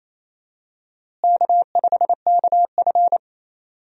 Morse code, heard as a keyed tone, about 27 wpm, 700 Hz. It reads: K5KF
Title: USA Call Signs - 4 Characters - Repeated 3x - 28wpm